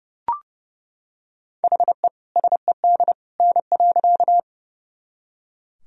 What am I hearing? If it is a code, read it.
Morse code, heard as a keyed tone, about 30 wpm, 700 Hz. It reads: HE SED N.
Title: Sentences from Top 100 Words and Abbreviations 30wpm